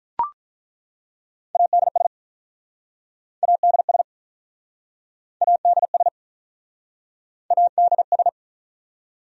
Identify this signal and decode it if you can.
Morse code, heard as a keyed tone, about 48 wpm, 700 Hz. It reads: ADS ADS ADS ADS